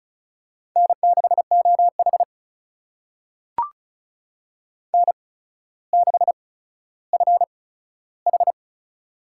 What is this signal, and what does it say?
Morse code, heard as a keyed tone, about 36 wpm, spaced as for 15 wpm, 700 Hz. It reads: N6OH N6FH
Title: USA Call Signs - 8x Character Spacing - 35wpm